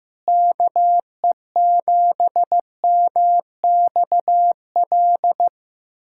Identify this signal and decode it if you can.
Morse code, heard as a keyed tone, about 15 wpm, 700 Hz. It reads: KE7MXL
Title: USA Callsign DE Callsign 15wpm